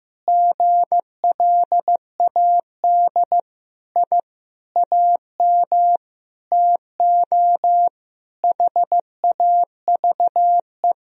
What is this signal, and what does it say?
Morse code, heard as a keyed tone, about 15 wpm, 700 Hz. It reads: GLAD I AM TO HAVE
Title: Sentences from Top 1000 Words - Review Part 2 - 15wpm